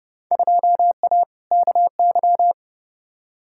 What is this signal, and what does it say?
Morse code, heard as a keyed tone, about 30 wpm, 700 Hz. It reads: 2A KY
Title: Field Day Exchanges - 30wpm